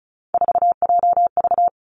Morse code, heard as a keyed tone, about 36 wpm, 700 Hz. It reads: 4JV
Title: Three Challenging Letter-Number Combinations 35wpm